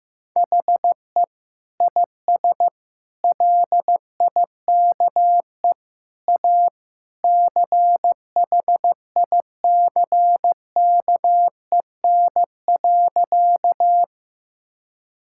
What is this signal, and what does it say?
Morse code, heard as a keyed tone, about 16 wpm, 700 Hz. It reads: HE IS LIKE A CHICKEN.
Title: Sentences from Top 1000 Words - Review Part 5 - 15wpm